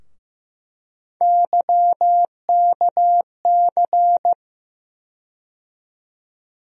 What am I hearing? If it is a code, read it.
Morse code, heard as a keyed tone, about 15 wpm, 700 Hz. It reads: YKC
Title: USA Call Sign Postfixes - 3 Letters 15wpm